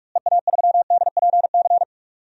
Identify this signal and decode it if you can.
Morse code, heard as a keyed tone, about 43 wpm, 700 Hz. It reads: EA3DPC
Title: DXCC-100 Call Signs - 6 Characters - Repeated 3x - 45wpm